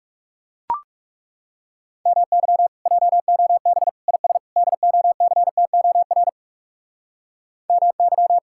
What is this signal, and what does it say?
Morse code, heard as a keyed tone, about 40 wpm, 700 Hz. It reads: MY JOB IS DOCTOR MY